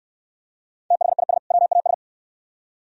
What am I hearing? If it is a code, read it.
Morse code, heard as a keyed tone, about 64 wpm, 700 Hz. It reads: THIS WAS